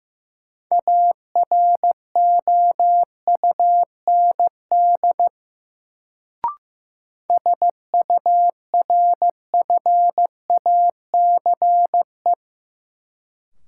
Morse code, heard as a keyed tone, about 15 wpm, 700 Hz. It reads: AROUND SURFACE
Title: Sentences from Top 600 Words - New Word Drill - 15wpm